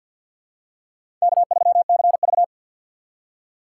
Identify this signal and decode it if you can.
Morse code, heard as a keyed tone, about 51 wpm, 700 Hz. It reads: K3CV
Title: USA Call Signs - 4 Characters - 3x Character Spacing - 50wpm